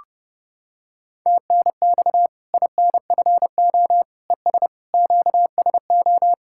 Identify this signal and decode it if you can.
Morse code, heard as a keyed tone, about 30 wpm, 700 Hz. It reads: TNX INFO ES QSO
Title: Practice QSO - Exchange 4 - Ending - 30wpm